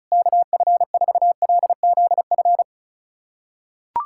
Morse code, heard as a keyed tone, about 36 wpm, 700 Hz. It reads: KF4LZF
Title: USA Callsign DE Callsign 35wpm